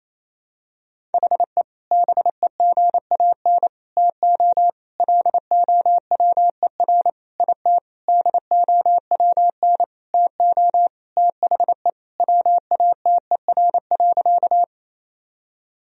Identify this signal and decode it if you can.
Morse code, heard as a keyed tone, about 28 wpm, 700 Hz. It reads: HE BEGAN TO LOWER IT DOWN TO THE WATER.
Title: Sentences from Top 1000 Words - Review Part 4 - 28wpm